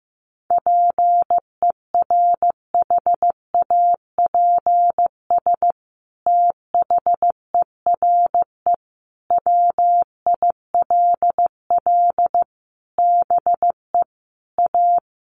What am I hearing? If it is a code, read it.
Morse code, heard as a keyed tone, about 15 wpm, 700 Hz. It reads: PERHAPS THERE WILL BE A
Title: Sentences from Top 1400 Words 15wpm